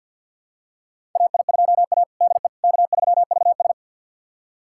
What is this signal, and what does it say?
Morse code, heard as a keyed tone, about 50 wpm, 700 Hz. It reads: AI2A DE K3VS
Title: USA Callsign DE Callsign - 4 Characters - 50wpm